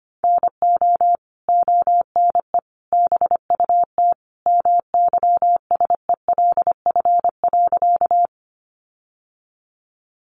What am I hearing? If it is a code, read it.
Morse code, heard as a keyed tone, about 25 wpm, 700 Hz. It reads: NO ONE BUT MYSELF.
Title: Sentences from Top 1300 Words 25wpm